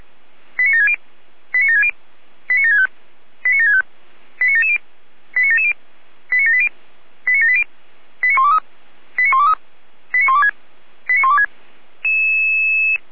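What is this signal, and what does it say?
Five-tone selective call (ZVEI), heard as quick runs of five tones, tones of about 80 ms. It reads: 89760 89760 89765 89765 89770 89770 89780 89780 89123 89123 89126 89126
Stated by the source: Five-tone ZVEI selective call 89755, received by SDR.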